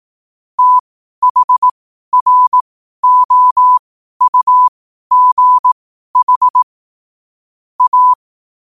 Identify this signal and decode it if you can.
Morse code, heard as a keyed tone, about 18 wpm, spaced as for 12 wpm, 1000 Hz. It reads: THROUGH A
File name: qotdfiles / 12wpm